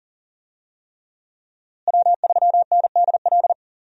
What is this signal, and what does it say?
Morse code, heard as a keyed tone, about 40 wpm, 700 Hz. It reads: W3NDL
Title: USA Call Signs - 5 Characters - Repeated 3x - 40wpm